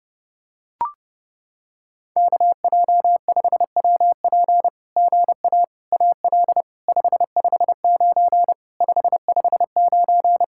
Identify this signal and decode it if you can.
Morse code, heard as a keyed tone, about 30 wpm, 700 Hz. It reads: KJ5WP GA AL 559 559
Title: SOTA QSO Practice 30wpm